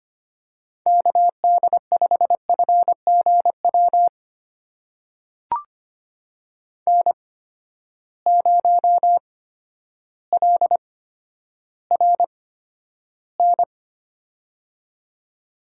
Morse code, heard as a keyed tone, about 25 wpm, spaced as for 11 wpm, 700 Hz. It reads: KD5FGW N0LRN
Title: USA Call Signs - 8x Character Spacing - 25wpm